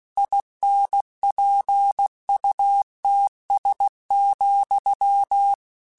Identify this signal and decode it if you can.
Morse code, heard as a keyed tone, about 16 wpm, 800 Hz. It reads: INPUTS,
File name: Technical Subjects / Technical Subjects-mp3s / AND Gate @15